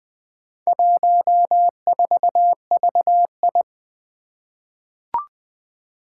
Morse code, heard as a keyed tone, about 20 wpm, 700 Hz. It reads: 14VI
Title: Four Challenging Letter-Number Combinations 20wpm